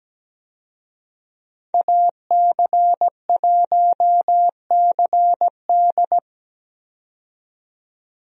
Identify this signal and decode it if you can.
Morse code, heard as a keyed tone, about 17 wpm, 700 Hz. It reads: AC1CD
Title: USA Call Signs - 5 Characters - 17wpm